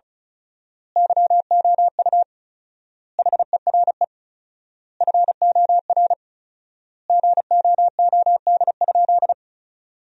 Morse code, heard as a keyed tone, about 34 wpm, 700 Hz. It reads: YOU HERE FOR GOOD?
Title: 4x Word-Spacing - Sentences from Top 200 Words - 35wpm